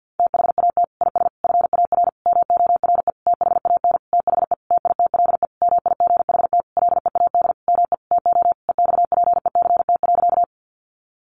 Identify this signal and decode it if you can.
Morse code, heard as a keyed tone, about 48 wpm, 700 Hz. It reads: THAT IS FAR MORE THAN THE TITLE MIGHT LEAD ME TO EXPECT.
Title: Sentences from Top 1200 Words 50wpm